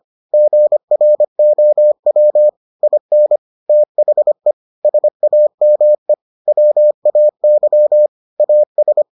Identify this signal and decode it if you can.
Morse code, heard as a keyed tone, about 25 wpm, 600 Hz. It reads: GROW IN THE SAME WAY AS